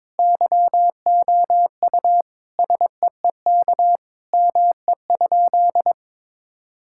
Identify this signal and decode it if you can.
Morse code, heard as a keyed tone, about 22 wpm, 700 Hz. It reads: YOU SEEK ME?
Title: Sentences from Top 1600 Words 22wpm